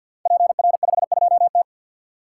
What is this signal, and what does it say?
Morse code, heard as a keyed tone, about 48 wpm, 690 Hz. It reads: PR52T
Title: Code Groups - 5 Letter:Numbers - 50wpm